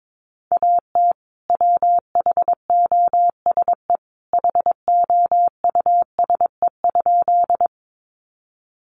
Morse code, heard as a keyed tone, about 22 wpm, 700 Hz. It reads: AT WHOSE HOUSE?
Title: Sentences from Top 1200 Words 22wpm